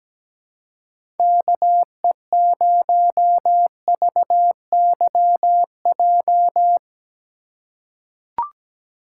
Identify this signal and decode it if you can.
Morse code, heard as a keyed tone, about 17 wpm, 700 Hz. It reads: KE0VYJ
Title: USA Call Signs - 6 Characters - 4x Character Spacing - 17wpm